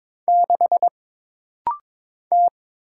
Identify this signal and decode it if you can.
Morse code, heard as a keyed tone, about 22 wpm, 700 Hz. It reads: 6 T
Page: This rapid-fire timing is geared to push your ICR (Instant Character Recognition) skills.